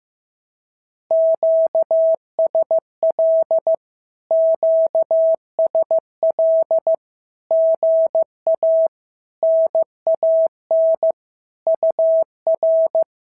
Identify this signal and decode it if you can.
Morse code, heard as a keyed tone, about 14 wpm, 650 Hz. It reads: QSL QSL GA NAN UR
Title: SOTA QSO Practice 15wpm